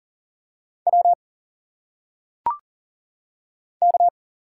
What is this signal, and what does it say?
Morse code, heard as a keyed tone, about 40 wpm, 700 Hz. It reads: W K
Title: Single Letters - Phonetic Pronunciation 40wpm